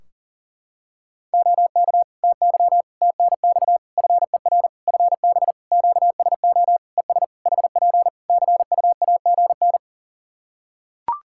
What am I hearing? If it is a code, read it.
Morse code, heard as a keyed tone, about 40 wpm, 700 Hz. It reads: OK TY TNX FER FB QSO ES HP CUAGN
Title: K1USN SST QSO Practice 40wpm